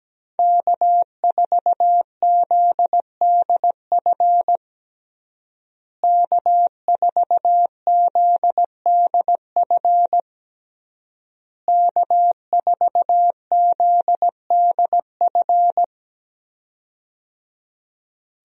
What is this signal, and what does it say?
Morse code, heard as a keyed tone, about 17 wpm, 700 Hz. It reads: K4ZDF K4ZDF K4ZDF
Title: USA Call Signs - 5 Characters - Repeated 3x - 17wpm